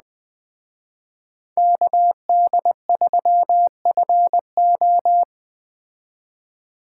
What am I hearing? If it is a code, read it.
Morse code, heard as a keyed tone, about 20 wpm, 700 Hz. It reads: KD3FO